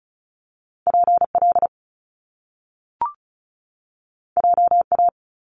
Morse code, heard as a keyed tone, about 36 wpm, 700 Hz. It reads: PL JA
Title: Top Two Letter Combinations in Words 35wpm